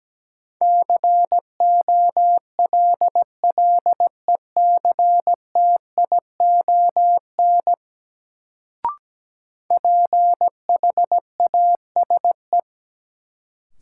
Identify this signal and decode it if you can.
Morse code, heard as a keyed tone, about 17 wpm, 700 Hz. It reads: COLLECTION PHASE
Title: Sentences from Top 1200 Words - New Word Drill - 17wpm